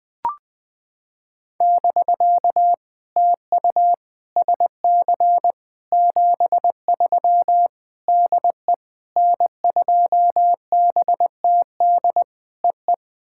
Morse code, heard as a keyed tone, about 19 wpm, 700 Hz. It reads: <BK> TU SC 73 DE N2BTD EE
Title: POTA QSO Practice - 20wpm